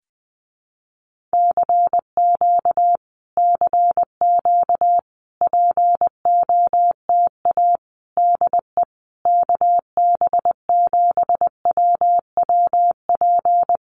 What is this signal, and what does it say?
Morse code, heard as a keyed tone, about 20 wpm, 700 Hz. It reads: CQ CQ POTA DE KB7WWP
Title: Common QSO Phrases Encore 20wpm